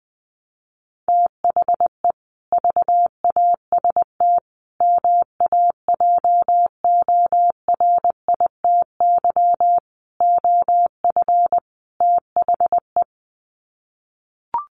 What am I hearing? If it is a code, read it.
Morse code, heard as a keyed tone, about 20 wpm, 700 Hz. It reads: THE VAST MAJORITY OF THE
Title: Sets of 5 Words Encore 20wpm